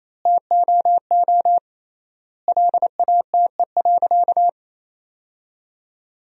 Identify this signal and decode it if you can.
Morse code, heard as a keyed tone, about 28 wpm, 700 Hz. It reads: TOO LATE.
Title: 3x Word-Spacing - Sentences from Top 300 Words - 28wpm